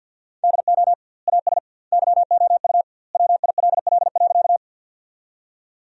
Morse code, heard as a keyed tone, about 50 wpm, 700 Hz. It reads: DO AS YOU WILL.
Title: 2x Word-Spacing - Sentences from Top 100 Words - 50wpm